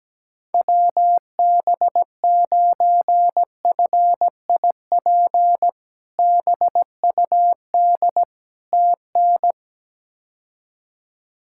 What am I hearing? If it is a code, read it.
Morse code, heard as a keyed tone, about 17 wpm, 700 Hz. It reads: WB9FIP BUD TN
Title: K1USN SST QSO Practice - 17wpm